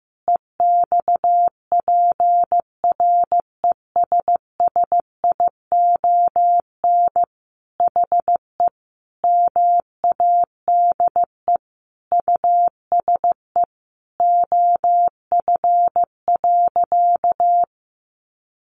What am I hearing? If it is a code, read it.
Morse code, heard as a keyed tone, about 15 wpm, 700 Hz. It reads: EXPRESSION HE MADE USE OF.
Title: Sentences from Top 1200 Words 15wpm